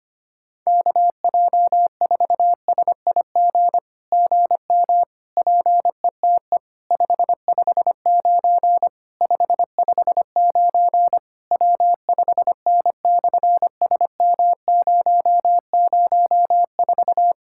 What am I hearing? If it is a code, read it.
Morse code, heard as a keyed tone, about 24 wpm, 700 Hz. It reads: KJ4SIG GM PETE 559 559 W5N/SM004
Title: SOTA QSO Practice 25wpm